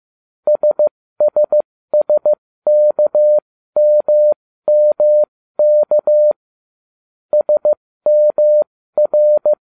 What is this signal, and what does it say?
Morse code, heard as a keyed tone, about 15 wpm, spaced as for 13 wpm, 600 Hz. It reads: SSSKMMK SMR